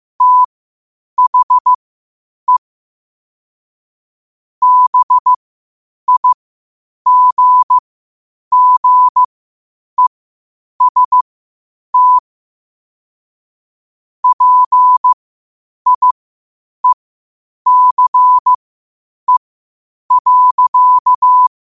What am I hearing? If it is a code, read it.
Morse code, heard as a keyed tone, about 15 wpm, spaced as for 8 wpm, 1000 Hz. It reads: THE BIGGEST PIECE.
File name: qotdfiles / 8wpm